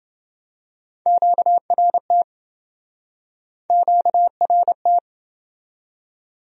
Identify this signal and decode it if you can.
Morse code, heard as a keyed tone, about 27 wpm, 700 Hz. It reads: QRT QRT